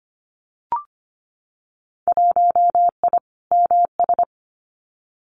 Morse code, heard as a keyed tone, about 25 wpm, 700 Hz. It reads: 1I MS
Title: Winter Field Day Exchanges Encore 25wpm